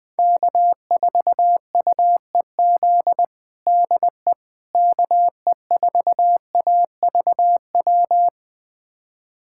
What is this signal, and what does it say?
Morse code, heard as a keyed tone, about 20 wpm, 700 Hz. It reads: K4UEZ DE KE4AVW
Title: USA Callsign DE Callsign 20wpm